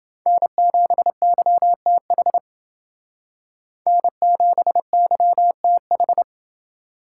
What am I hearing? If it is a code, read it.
Morse code, heard as a keyed tone, about 28 wpm, 700 Hz. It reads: N7YTH N7YTH